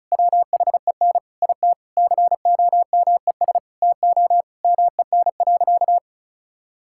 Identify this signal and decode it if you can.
Morse code, heard as a keyed tone, about 34 wpm, 700 Hz. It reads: WHEN IT COMES TO MEN.
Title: Sentences from Top 1000 Words - Review Part 6 - 35wpm